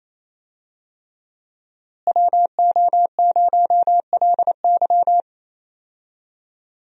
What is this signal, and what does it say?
Morse code, heard as a keyed tone, about 27 wpm, 700 Hz. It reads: WO0LY
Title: USA Call Signs - 5 Characters - Repeated 3x - 28wpm